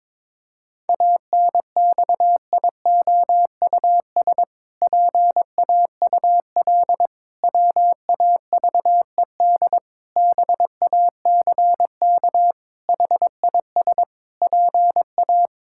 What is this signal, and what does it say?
Morse code, heard as a keyed tone, about 21 wpm, 700 Hz. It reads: ANXIOUS PAUL WAVED BACK HIS PA
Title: Pangram Challenge v2 - 22wpm